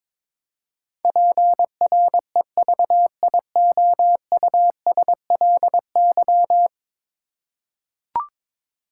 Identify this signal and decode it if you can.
Morse code, heard as a keyed tone, about 22 wpm, 700 Hz. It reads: PREVIOUSLY
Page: morse-code-ninja-practice-22wpm